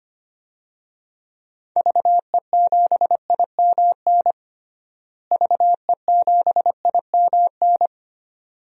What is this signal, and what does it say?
Morse code, heard as a keyed tone, about 25 wpm, 700 Hz. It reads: VE7IMN VE7IMN
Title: DXCC-100 Call Signs - 6 Characters - Repeated 2x - 25wpm